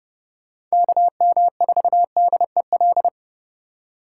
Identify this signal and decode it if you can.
Morse code, heard as a keyed tone, about 36 wpm, 700 Hz. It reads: KM4DEL
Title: Speed Racing - USA 2x3 Call Signs 30wpm